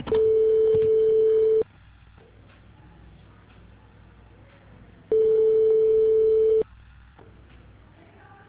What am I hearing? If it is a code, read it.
Ringback tone (France), 440 Hz.